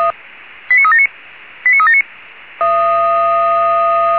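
Five-tone selective call (ZVEI), heard as quick runs of five tones, tones of about 70 ms. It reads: 89379 89379
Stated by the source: ZVEI five-tone selective call received by SDR.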